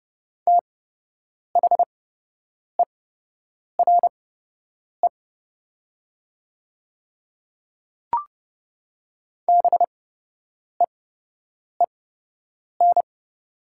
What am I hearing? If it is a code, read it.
Morse code, heard as a keyed tone, about 30 wpm, spaced as for 8 wpm, 700 Hz. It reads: THERE BEEN
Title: 8x Character Spacing - Top 100 Words - 30wpm